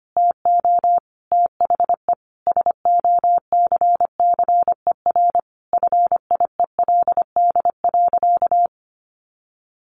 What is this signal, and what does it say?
Morse code, heard as a keyed tone, about 25 wpm, 700 Hz. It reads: TO THE SOCCER FIELD.
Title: Easy Phrases 25wpm